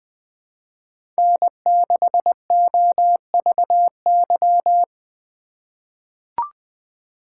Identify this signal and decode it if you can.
Morse code, heard as a keyed tone, about 20 wpm, 700 Hz. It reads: N6OVY
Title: USA Call Signs - 5 Characters - 4x Character Spacing - 20wpm